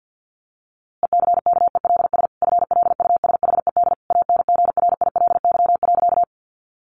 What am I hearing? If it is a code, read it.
Morse code, heard as a keyed tone, about 50 wpm, 700 Hz. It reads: ECKELS FLUSHED ANGRILY.
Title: A Sound Of Thunder 50wpm